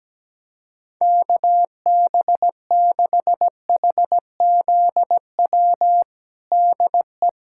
Morse code, heard as a keyed tone, about 17 wpm, 700 Hz. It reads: KB6HZW DE
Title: USA Callsign DE Callsign - 6 Characters - 17wpm